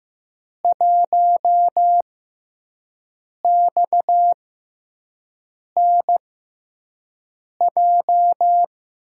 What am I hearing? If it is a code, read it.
Morse code, heard as a keyed tone, about 15 wpm, spaced as for 5 wpm, 700 Hz. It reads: 1XNJ
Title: USA Call Signs - 5 Characters - 6x Character Spacing - 15wpm